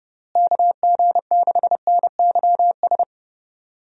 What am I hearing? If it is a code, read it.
Morse code, heard as a keyed tone, about 30 wpm, 700 Hz. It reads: KG6NYS
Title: USA Call Signs - 6 Characters - Repeated 3x - 30wpm